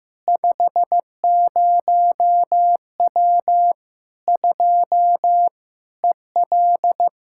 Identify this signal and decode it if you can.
Morse code, heard as a keyed tone, about 15 wpm, 700 Hz. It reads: 50W 2 EL
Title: K1USN SST QSO Practice 15wpm